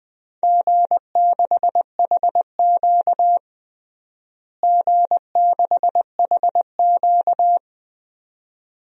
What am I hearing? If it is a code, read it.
Morse code, heard as a keyed tone, about 20 wpm, 700 Hz. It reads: G6HQ G6HQ
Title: DXCC-100 Call Signs - 4 Characters - Repeated 2x - 20wpm